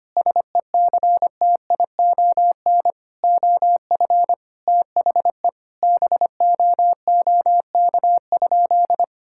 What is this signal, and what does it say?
Morse code, heard as a keyed tone, about 25 wpm, 700 Hz. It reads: SECTION OF THE BOOK?
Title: Sentences from Top 1000 Words - Review Part 5 - 25wpm